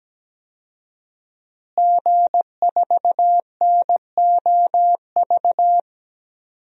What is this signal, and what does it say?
Morse code, heard as a keyed tone, about 17 wpm, 700 Hz. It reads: G4NOV
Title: DXCC-100 Call Signs - 5 Characters - Repeated 2x - 17wpm